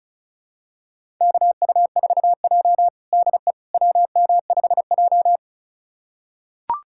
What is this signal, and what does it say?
Morse code, heard as a keyed tone, about 36 wpm, 700 Hz. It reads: KU4J DE WM5J
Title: USA Callsign DE Callsign - 4 Characters - 35wpm